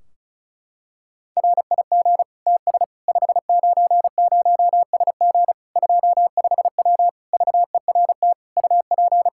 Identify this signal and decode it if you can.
Morse code, heard as a keyed tone, about 36 wpm, 700 Hz. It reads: RIG TS 590SG 25W VERT UP
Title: Practice QSO - Exchange 2 - Info - 35wpm